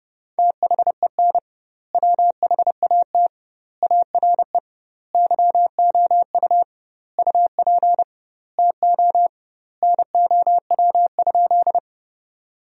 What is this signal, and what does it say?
Morse code, heard as a keyed tone, about 32 wpm, 700 Hz. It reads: THEN WHAT ARE YOU UP TO NOW?
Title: 2x Word-Spacing - Sentences from Top 100 Words - 30wpm